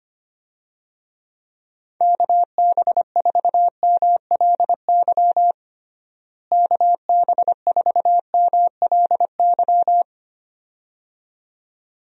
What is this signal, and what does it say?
Morse code, heard as a keyed tone, about 25 wpm, 700 Hz. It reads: KB4MLY KB4MLY
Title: USA Call Signs - 6 Characters - Repeated 2x - 25wpm